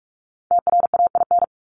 Morse code, heard as a keyed tone, about 44 wpm, 700 Hz. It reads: TRAIN
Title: ING Suffix - Top 100 Words with Base Words 45wpm